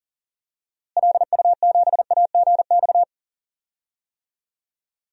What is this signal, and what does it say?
Morse code, heard as a keyed tone, about 40 wpm, 700 Hz. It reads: LU7AGX
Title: DXCC-100 Call Signs - 6 Characters - Repeated 3x - 40wpm